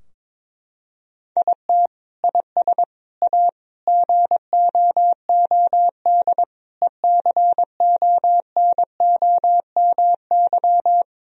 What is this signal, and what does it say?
Morse code, heard as a keyed tone, about 22 wpm, 700 Hz. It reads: IT IS A GOOD ECONOMY
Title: Sentences from Top 1000 Words - Review Part 3 - 22wpm